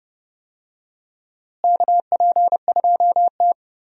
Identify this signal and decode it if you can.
Morse code, heard as a keyed tone, about 30 wpm, 700 Hz. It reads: KP2T
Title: USA Call Signs - 4 Characters - Repeated 3x - 30wpm